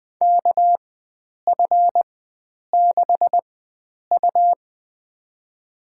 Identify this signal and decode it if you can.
Morse code, heard as a keyed tone, about 20 wpm, spaced as for 9 wpm, 700 Hz. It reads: KF6U
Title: USA Call Signs - 4 Characters - 4x Character Spacing - 20wpm